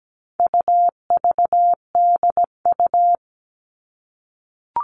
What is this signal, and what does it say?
Morse code, heard as a keyed tone, about 17 wpm, 700 Hz. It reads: UVDU
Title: All Cut Numbers - 4-Digit 17wpm